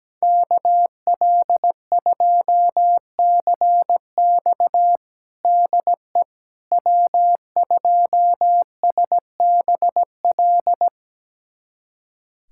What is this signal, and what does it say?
Morse code, heard as a keyed tone, about 17 wpm, 700 Hz. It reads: KL2CX DE W2SBL
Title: USA Callsign DE Callsign - 5 Characters - 17wpm